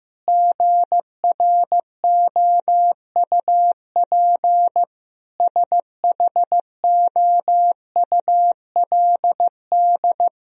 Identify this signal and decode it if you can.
Morse code, heard as a keyed tone, about 15 wpm, 700 Hz. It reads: GROUP SHOULD
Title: Sentences from Top 1000 Words - Review Part 1 - 15wpm